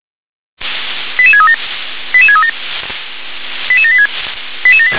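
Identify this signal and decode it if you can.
Five-tone selective call (ZVEI), heard as quick runs of five tones, tones of about 70 ms. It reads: 88637 88637 88786
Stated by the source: Received on SDR, ZVEI five-tone call.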